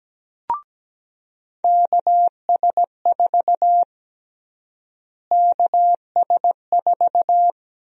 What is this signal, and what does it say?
Morse code, heard as a keyed tone, about 17 wpm, 700 Hz. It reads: KS4 KS4
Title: USA Call Sign Prefixes - Repeated 2x - 17wpm